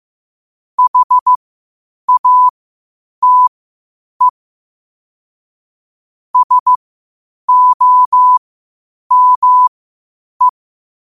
Morse code, heard as a keyed tone, about 15 wpm, spaced as for 8 wpm, 1000 Hz. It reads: HATE SOME
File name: qotdfiles / 8wpm